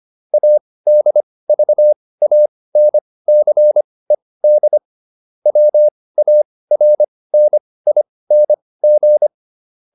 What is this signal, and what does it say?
Morse code, heard as a keyed tone, about 25 wpm, spaced as for 18 wpm, 600 Hz. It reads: ADVANCED WARNING